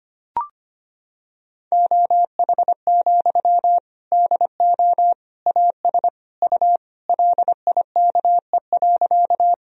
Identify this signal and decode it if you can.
Morse code, heard as a keyed tone, about 25 wpm, 700 Hz. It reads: OH, DO AS U LIKE.
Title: Sentences from Top 500 Words and Abbreviations 25wpm